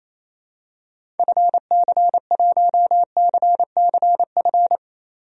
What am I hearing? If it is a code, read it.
Morse code, heard as a keyed tone, about 28 wpm, 700 Hz. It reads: FC1CCF
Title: DXCC-100 Call Signs - 6 Characters - 28wpm